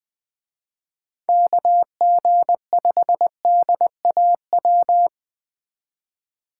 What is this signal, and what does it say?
Morse code, heard as a keyed tone, about 20 wpm, 700 Hz. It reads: KG5DAW
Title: Speed Racing - USA 2x3 Call Signs 17wpm